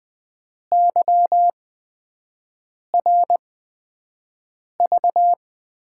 Morse code, heard as a keyed tone, about 20 wpm, spaced as for 5 wpm, 700 Hz. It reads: YRV
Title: USA Call Signs - 6 Characters - 8x Character Spacing - 20wpm